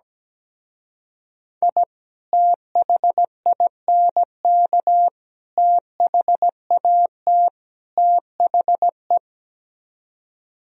Morse code, heard as a keyed tone, about 17 wpm, 700 Hz. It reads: I THINK THAT THE
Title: Speed-Racing - Sets of 4 Words 17wpm